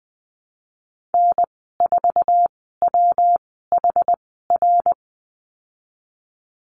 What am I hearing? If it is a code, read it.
Morse code, heard as a keyed tone, about 20 wpm, spaced as for 15 wpm, 700 Hz. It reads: N4WHR
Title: 128 Lesson 21g - Number 6 - Call Signs 20wpm